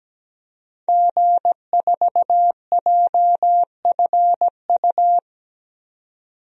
Morse code, heard as a keyed tone, about 17 wpm, 700 Hz. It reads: G4JFU
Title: DXCC-100 Call Signs - 5 Characters - 17wpm